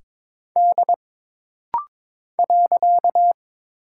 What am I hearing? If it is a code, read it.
Morse code, heard as a keyed tone, about 22 wpm, 700 Hz. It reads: D .
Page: This rapid-fire timing is geared to push your ICR (Instant Character Recognition) skills.